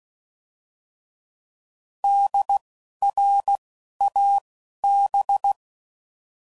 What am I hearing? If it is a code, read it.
Morse code, heard as a keyed tone, about 16 wpm, 800 Hz. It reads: D R A B
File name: Prefixes / By First Letter / Ds